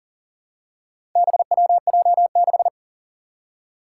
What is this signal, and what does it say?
Morse code, heard as a keyed tone, about 40 wpm, 700 Hz. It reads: BWJ6
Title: Four Challenging Letter-Number Combinations 40wpm